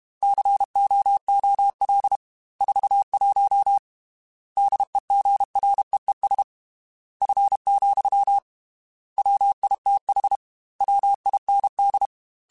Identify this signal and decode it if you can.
Morse code, heard as a keyed tone, about 32 wpm, 800 Hz. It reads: COOL 41 DEGREES F, WITH WIND
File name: Weather Sets / Weather Sets-mp3s / Weather 4 14 @30